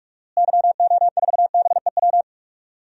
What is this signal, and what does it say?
Morse code, heard as a keyed tone, about 46 wpm, 700 Hz. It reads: YO4BEW
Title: DXCC-100 Call Signs - 6 Characters - Repeated 3x - 45wpm